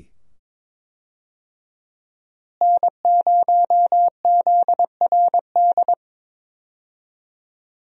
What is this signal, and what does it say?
Morse code, heard as a keyed tone, about 21 wpm, 700 Hz. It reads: N0ZRD
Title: USA Call Signs - 5 Characters - Repeated 2x - 22wpm